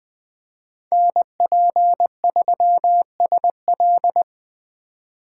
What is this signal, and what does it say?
Morse code, heard as a keyed tone, about 20 wpm, 700 Hz. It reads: NP3SL
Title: USA Call Signs - 5 Characters - 3x Character Spacing - 20wpm